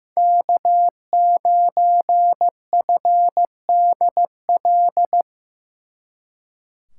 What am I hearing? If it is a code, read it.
Morse code, heard as a keyed tone, about 18 wpm, 700 Hz. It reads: K9FDL
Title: Speed Racing - USA 1x3 Call Signs 15wpm